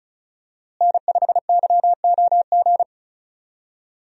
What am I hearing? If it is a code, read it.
Morse code, heard as a keyed tone, about 34 wpm, 700 Hz. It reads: N5YOG
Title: USA Call Signs - 5 Characters - Repeated 2x - 35wpm